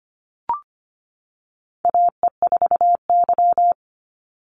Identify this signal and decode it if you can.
Morse code, heard as a keyed tone, about 25 wpm, 700 Hz. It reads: AE4Y
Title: NA SOTA Chaser Callsigns 25wpm